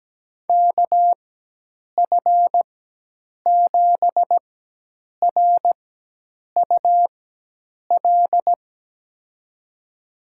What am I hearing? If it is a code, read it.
Morse code, heard as a keyed tone, about 17 wpm, spaced as for 8 wpm, 700 Hz. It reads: KF7RUL
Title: USA Call Signs - 6 Characters - 4x Character Spacing - 17wpm